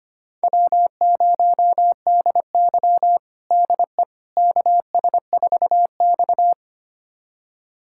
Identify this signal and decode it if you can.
Morse code, heard as a keyed tone, about 25 wpm, 700 Hz. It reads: W0DY DE KS4X
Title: USA Callsign DE Callsign - 4 Characters - 25wpm